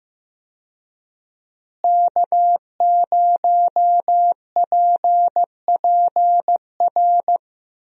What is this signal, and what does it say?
Morse code, heard as a keyed tone, about 15 wpm, 700 Hz. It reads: K0PPR
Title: USA Call Signs - 5 Characters - Repeated 3x - 15wpm